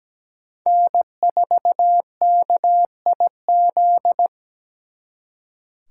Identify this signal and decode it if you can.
Morse code, heard as a keyed tone, about 20 wpm, 700 Hz. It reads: N4KIZ
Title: Speed Racing - USA 1x3 Call Signs 17wpm